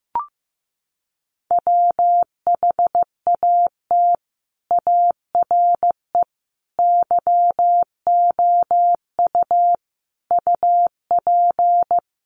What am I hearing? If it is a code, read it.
Morse code, heard as a keyed tone, about 15 wpm, 700 Hz. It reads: WHAT ARE YOU UP
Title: Sentences from Top 100 Words 15wpm